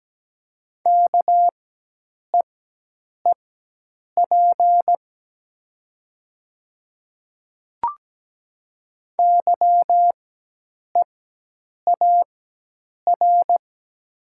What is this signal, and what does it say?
Morse code, heard as a keyed tone, about 17 wpm, spaced as for 8 wpm, 700 Hz. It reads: KEEP YEAR
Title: 4x Character Spacing - Top 300 Words - 17wpm